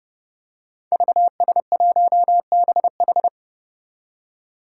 Morse code, heard as a keyed tone, about 29 wpm, 700 Hz. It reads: VS1BH
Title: Five Challenging Letter-Number Combinations 30wpm